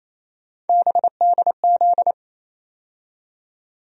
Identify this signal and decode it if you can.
Morse code, heard as a keyed tone, about 28 wpm, 700 Hz. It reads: BDZ
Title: USA Call Sign Postfixes - 3 Letters 28wpm